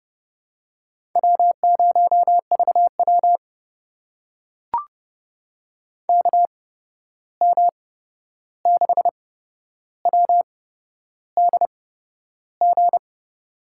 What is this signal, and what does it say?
Morse code, heard as a keyed tone, about 30 wpm, spaced as for 11 wpm, 700 Hz. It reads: W0VW KM6WDG
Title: USA Call Signs - 8x Character Spacing - 30wpm